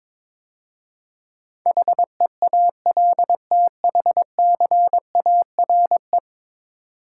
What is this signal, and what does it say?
Morse code, heard as a keyed tone, about 22 wpm, 700 Hz. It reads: HEALTHCARE
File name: practice / 22wpm / 10 Letter Words - First 3 Letters Given - 22wpm